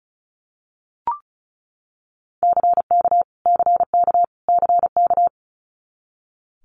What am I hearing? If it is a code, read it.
Morse code, heard as a keyed tone, about 36 wpm, 700 Hz. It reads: CK CK CK
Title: Multi-letter Phonemes Repeated 3x 35wpm